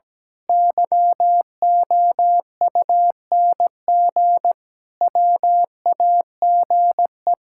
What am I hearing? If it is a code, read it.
Morse code, heard as a keyed tone, about 17 wpm, 700 Hz. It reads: YOUNG WAGE
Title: Pangram Challenge v2 - 17wpm